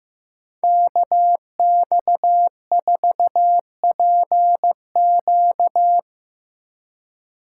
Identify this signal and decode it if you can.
Morse code, heard as a keyed tone, about 15 wpm, 700 Hz. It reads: KX4PQ
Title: K1USN SST QSO Practice - 15wpm